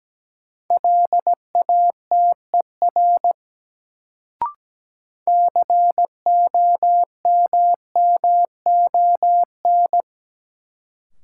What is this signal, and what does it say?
Morse code, heard as a keyed tone, about 17 wpm, 700 Hz. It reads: LATER COMMON
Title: Sentences from Top 700 Words - New Word Drill - 17wpm